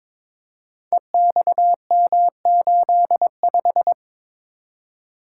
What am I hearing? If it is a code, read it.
Morse code, heard as a keyed tone, about 22 wpm, 700 Hz. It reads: EXM85
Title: Code Groups - 5 Letter:Numbers - 22wpm